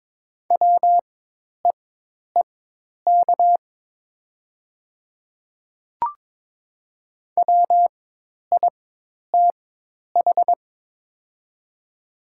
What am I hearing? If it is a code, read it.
Morse code, heard as a keyed tone, about 21 wpm, spaced as for 10 wpm, 700 Hz. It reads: WEEK WITH
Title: 4x Character Spacing - Top 500 Words - 22wpm